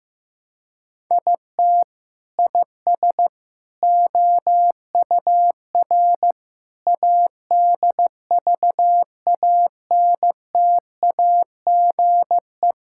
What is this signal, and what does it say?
Morse code, heard as a keyed tone, about 15 wpm, 700 Hz. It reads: IT IS OUR ADVANTAGE
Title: Sentences from Top 1000 Words - Review Part 6 - 15wpm